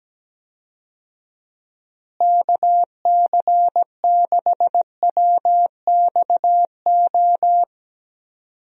Morse code, heard as a keyed tone, about 17 wpm, 700 Hz. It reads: KC6WXO
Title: USA Call Signs - 6 Characters - Repeated 2x - 17wpm